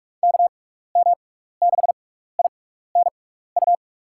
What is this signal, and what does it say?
Morse code, heard as a keyed tone, about 46 wpm, spaced as for 15 wpm, 700 Hz. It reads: KM6INU
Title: USA Call Signs - 6x Character Spacing - 45wpm